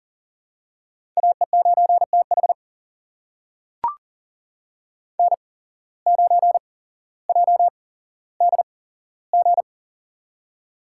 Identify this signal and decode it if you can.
Morse code, heard as a keyed tone, about 40 wpm, spaced as for 17 wpm, 700 Hz. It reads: AE9TH N9JDG
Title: USA Call Signs - 5 Characters - 8x Character Spacing - 40wpm